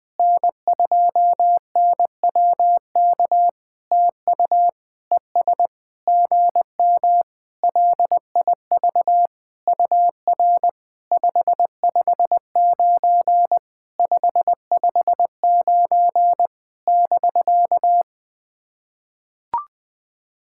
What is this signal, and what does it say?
Morse code, heard as a keyed tone, about 20 wpm, 700 Hz. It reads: N2NWK TU ES GM LIV UR 559 559 <BK>
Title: POTA QSO Practice - 20wpm